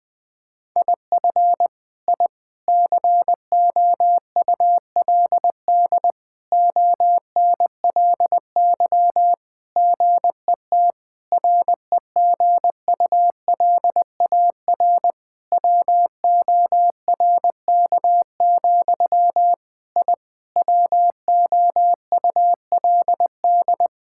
Morse code, heard as a keyed tone, about 20 wpm, 700 Hz. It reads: IF I COULD ONLY GET REGULAR WORK, I WOULD
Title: Sentences from Top 1000 Words - Review Part 6 - 20wpm